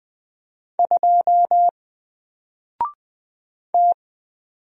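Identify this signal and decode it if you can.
Morse code, heard as a keyed tone, about 20 wpm, 700 Hz. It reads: 2 T
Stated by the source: Single letter and number Morse drill at 20 wpm, from N4LCD.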